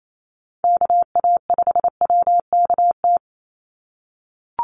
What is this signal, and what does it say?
Morse code, heard as a keyed tone, about 28 wpm, 700 Hz. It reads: KA5WKT
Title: USA Call Signs 28wpm